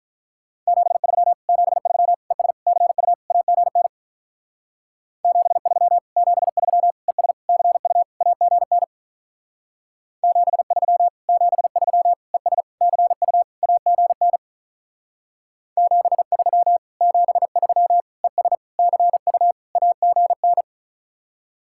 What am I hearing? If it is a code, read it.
Morse code, heard as a keyed tone, about 44 wpm, 700 Hz. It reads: 73 73 ES CU AGN 73 73 ES CU AGN 73 73 ES CU AGN 73 73 ES CU AGN